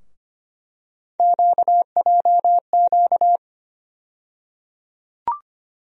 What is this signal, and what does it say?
Morse code, heard as a keyed tone, about 25 wpm, 700 Hz. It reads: QJQ
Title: USA Call Sign Postfixes - 3 Letters 25wpm